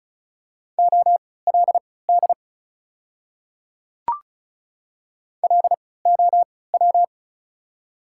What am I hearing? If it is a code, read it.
Morse code, heard as a keyed tone, about 34 wpm, spaced as for 20 wpm, 700 Hz. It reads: OLD LOW
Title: 3x Character Spacing - Top 300 Words - 35wpm